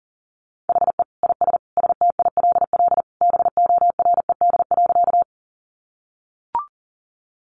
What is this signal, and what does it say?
Morse code, heard as a keyed tone, about 40 wpm, 700 Hz. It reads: HE IS STILL BORED.
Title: Easy Phrases 40wpm